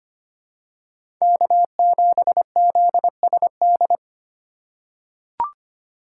Morse code, heard as a keyed tone, about 25 wpm, 700 Hz. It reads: K7ZSD
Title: USA Call Signs - 5 Characters - 3x Character Spacing - 25wpm